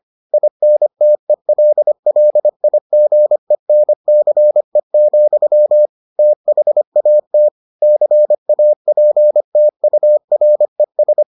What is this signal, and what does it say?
Morse code, heard as a keyed tone, about 25 wpm, 600 Hz. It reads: INTELLIGENCE, THAT CAPTURES